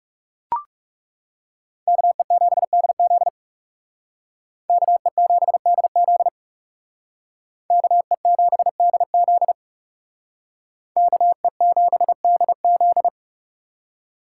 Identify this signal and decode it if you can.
Morse code, heard as a keyed tone, about 38 wpm, 700 Hz. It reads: KE7DZ KE7DZ KE7DZ KE7DZ